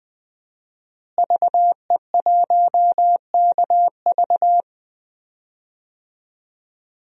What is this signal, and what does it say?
Morse code, heard as a keyed tone, about 20 wpm, 700 Hz. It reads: VE1KV
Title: DXCC-100 Call Signs - 5 Characters - Repeated 2x - 20wpm